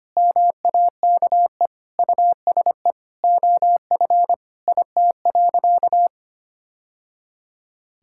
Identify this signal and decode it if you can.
Morse code, heard as a keyed tone, about 25 wpm, 700 Hz. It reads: MAKE USE OF IT.
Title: Sentences from Top 1000 Words - Review Part 6 - 25wpm